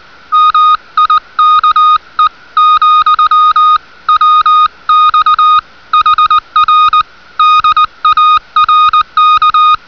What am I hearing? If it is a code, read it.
Morse code, heard as a keyed tone, about 20 wpm, 1240 Hz. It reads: MIKE , WX HR DARK